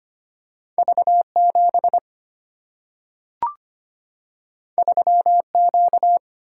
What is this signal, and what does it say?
Morse code, heard as a keyed tone, about 25 wpm, 700 Hz. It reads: V 7 3 Q
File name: Two Letter-Number Combinations - 25wpm